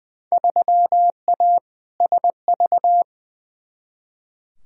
Morse code, heard as a keyed tone, about 20 wpm, 700 Hz. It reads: 3A SV
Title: Field Day Exchanges - 20wpm